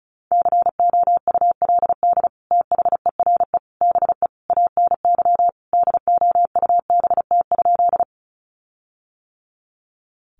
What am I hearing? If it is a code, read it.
Morse code, heard as a keyed tone, about 34 wpm, 700 Hz. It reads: COULD THERE BE ANY DOUBT?
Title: Sentences from Top 1500 Words 35wpm